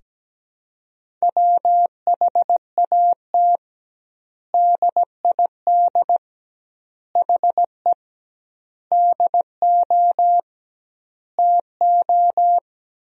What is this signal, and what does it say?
Morse code, heard as a keyed tone, about 17 wpm, 700 Hz. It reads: WHAT DID HE DO TO
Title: 2x Word-Spacing - Sentences from Top 100 Words - 17wpm